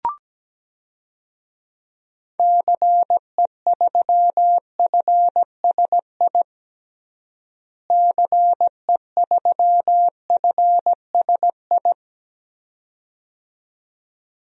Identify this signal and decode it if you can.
Morse code, heard as a keyed tone, about 17 wpm, 700 Hz. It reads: CE3FSI CE3FSI
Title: DXCC-100 Call Signs - 6 Characters - Repeated 2x - 17wpm